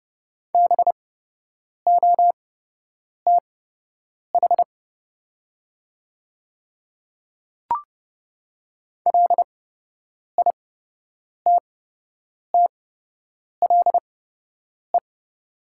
Morse code, heard as a keyed tone, about 30 wpm, spaced as for 8 wpm, 700 Hz. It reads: BOTH LITTLE
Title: 8x Character Spacing - Top 300 Words - 30wpm